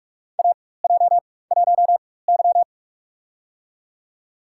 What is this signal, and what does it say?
Morse code, heard as a keyed tone, about 46 wpm, spaced as for 21 wpm, 700 Hz. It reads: AJ1Y
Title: USA Call Signs - 4 Characters - 4x Character Spacing - 45wpm